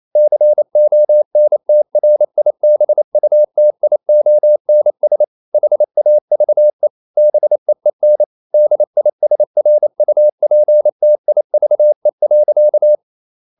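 Morse code, heard as a keyed tone, about 28 wpm, 600 Hz. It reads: CONTRIBUTIONS HAVE BEEN DISRUPTIVE.
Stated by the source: Morse code practice at 28 wpm, text of Ken Follett's novel Never in English, from CW med Gnister.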